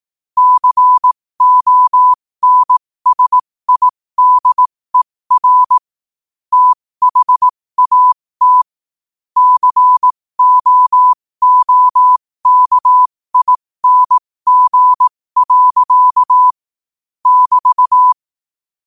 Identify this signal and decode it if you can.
Morse code, heard as a keyed tone, about 18 wpm, spaced as for 15 wpm, 1000 Hz. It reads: CONSIDER THAT COOKING. <BT>
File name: qotdfiles / 15wpm